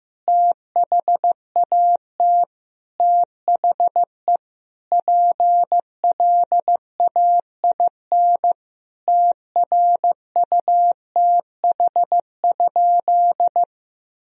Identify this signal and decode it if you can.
Morse code, heard as a keyed tone, about 15 wpm, 700 Hz. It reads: THAT THE PLAIN TRUTH?
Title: Sentences from Top 1000 Words - Review Part 2 - 15wpm